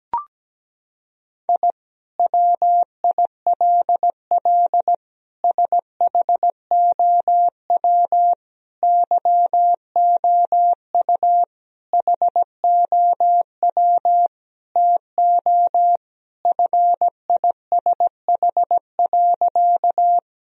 Morse code, heard as a keyed tone, about 17 wpm, 700 Hz. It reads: I WILL SHOW YOU HOW TO FISH.
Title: Sentences from Top 1000 Words - Review Part 1 - 17wpm